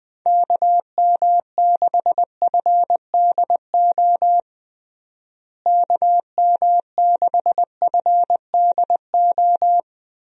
Morse code, heard as a keyed tone, about 20 wpm, 700 Hz. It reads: KM6FDO KM6FDO
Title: USA Call Signs - 6 Characters - Repeated 2x - 20wpm